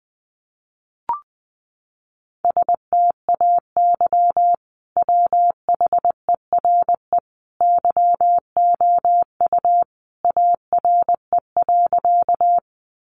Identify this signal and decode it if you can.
Morse code, heard as a keyed tone, about 20 wpm, 700 Hz. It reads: STAY WHERE YOU ARE.
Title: Sentences from Top 500 Words 20wpm